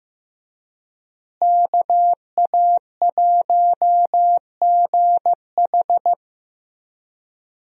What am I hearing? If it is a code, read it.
Morse code, heard as a keyed tone, about 15 wpm, 700 Hz. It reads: KA1GH
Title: USA Call Signs - 5 Characters - Repeated 3x - 15wpm